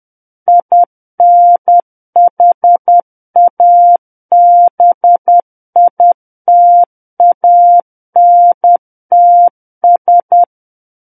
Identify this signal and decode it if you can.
Morse code, heard as a keyed tone, about 10 wpm, 700 Hz. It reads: INHABITANTS
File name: Book One / 01-Chapter 1 - 10 WPM